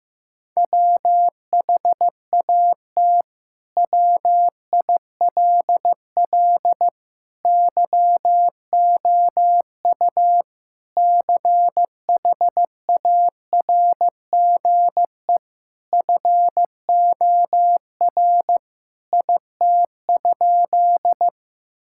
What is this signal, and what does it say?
Morse code, heard as a keyed tone, about 15 wpm, 700 Hz. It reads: WHAT WILL YOU CHARGE FOR IT?
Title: Sentences from Top 1000 Words - Review Part 6 - 15wpm